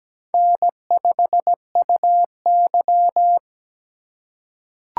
Morse code, heard as a keyed tone, about 17 wpm, 700 Hz. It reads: N5UY
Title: USA Call Signs - 4x Character Spacing - 17wpm